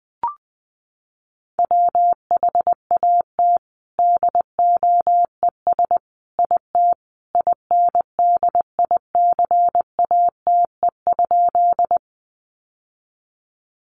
Morse code, heard as a keyed tone, about 20 wpm, 700 Hz. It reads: WHAT DOES IT INDICATE?
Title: Sentences from Top 1300 Words 20wpm